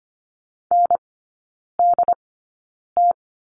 Morse code, heard as a keyed tone, about 25 wpm, spaced as for 9 wpm, 690 Hz. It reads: NDT
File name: Lesson 14e - Number 2 - Three Characters 25wpm